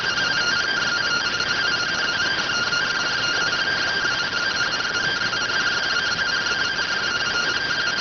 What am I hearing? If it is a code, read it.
Two-tone teleprinter signal, about 170 Hz shift.